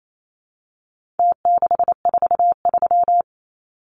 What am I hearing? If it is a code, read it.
Morse code, heard as a keyed tone, about 28 wpm, 700 Hz. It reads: T643
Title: Numbers - Contest Style - 4-Digit 28wpm